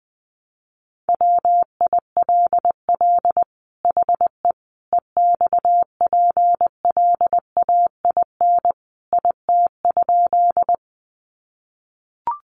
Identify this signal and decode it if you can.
Morse code, heard as a keyed tone, about 20 wpm, 700 Hz. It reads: WILL HE EXPLAIN IT?
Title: Sentences from Top 1100 Words 20wpm